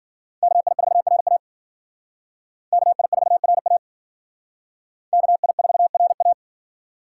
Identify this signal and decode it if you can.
Morse code, heard as a keyed tone, about 52 wpm, 700 Hz. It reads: KI4RA KI4RA KI4RA